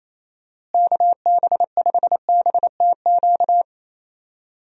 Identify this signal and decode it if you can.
Morse code, heard as a keyed tone, about 28 wpm, 700 Hz. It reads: KB5BTQ
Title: USA Call Signs - 6 Characters - Repeated 3x - 28wpm